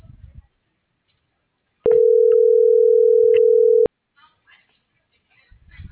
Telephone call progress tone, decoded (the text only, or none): ringback tone